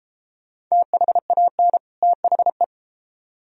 Morse code, heard as a keyed tone, about 39 wpm, 690 Hz. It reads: THAN THE